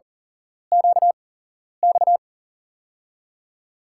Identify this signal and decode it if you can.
Morse code, heard as a keyed tone, about 42 wpm, spaced as for 11 wpm, 700 Hz. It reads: QX
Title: USA Call Signs - 8x Character Spacing - 40wpm